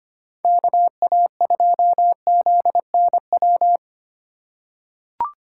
Morse code, heard as a keyed tone, about 25 wpm, 700 Hz. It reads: KA2ZNW
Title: USA Call Signs - 6 Characters - 2x Character Spacing - 25wpm